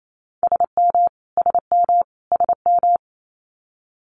Morse code, heard as a keyed tone, about 28 wpm, 700 Hz. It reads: SM SM SM
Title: Multi-letter Phonemes Repeated 3x 28wpm